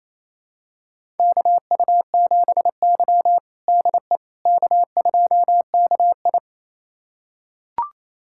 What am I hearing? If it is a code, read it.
Morse code, heard as a keyed tone, about 29 wpm, 690 Hz. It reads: KU7Y DE K2KI
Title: USA Callsign DE Callsign - 4 Characters - 28wpm